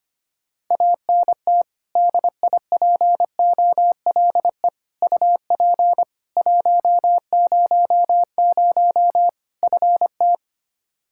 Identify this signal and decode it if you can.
Morse code, heard as a keyed tone, about 25 wpm, 700 Hz. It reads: ANT DIPOLE UP 100 FT
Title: Practice QSO - Exchange 2 - Info - 25wpm